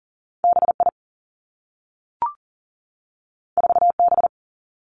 Morse code, heard as a keyed tone, about 38 wpm, 700 Hz. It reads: B I 4 B
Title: Two Challenging Letter-Number Combinations 40wpm